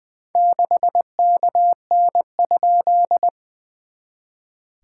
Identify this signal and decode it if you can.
Morse code, heard as a keyed tone, about 20 wpm, 690 Hz. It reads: 6KN?
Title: SOTA QSO Practice 20wpm